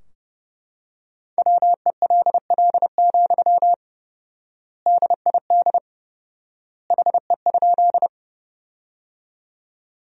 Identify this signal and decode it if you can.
Morse code, heard as a keyed tone, about 30 wpm, 700 Hz. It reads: WELL, DID HE?
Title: 4x Word-Spacing - Sentences from Top 200 Words - 30wpm